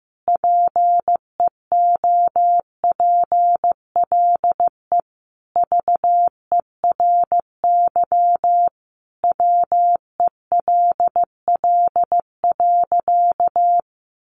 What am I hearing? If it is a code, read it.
Morse code, heard as a keyed tone, about 15 wpm, 700 Hz. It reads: PEOPLE VERY WELL.
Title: Sentences from Top 1300 Words 15wpm